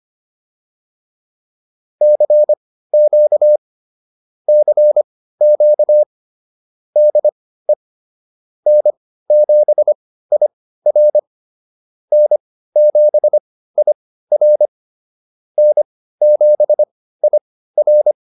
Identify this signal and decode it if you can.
Morse code, heard as a keyed tone, about 25 wpm, spaced as for 15 wpm, 600 Hz. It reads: CQ CQ DE N7IR N7IR N7IR